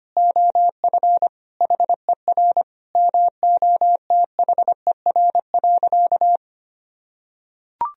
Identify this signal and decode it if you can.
Morse code, heard as a keyed tone, about 25 wpm, 700 Hz. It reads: OF HER MOTHER.
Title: Sentences from Top 1000 Words - Review Part 4 - 25wpm